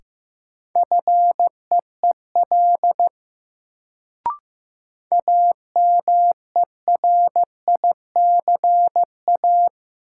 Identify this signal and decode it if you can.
Morse code, heard as a keyed tone, about 16 wpm, 700 Hz. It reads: FEEL AMERICA
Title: Sets of 5 Words - New Word Drill - 15wpm